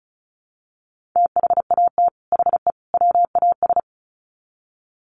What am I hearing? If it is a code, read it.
Morse code, heard as a keyed tone, about 34 wpm, 700 Hz. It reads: THAT HE WAS
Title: Sets of 3 Words 35wpm